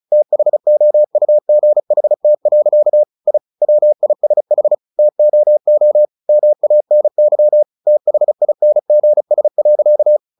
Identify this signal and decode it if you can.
Morse code, heard as a keyed tone, about 34 wpm, 600 Hz. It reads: THOUGHT. I WISH TOO MANY THINGS.
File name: the old man and the sea 2 35wpm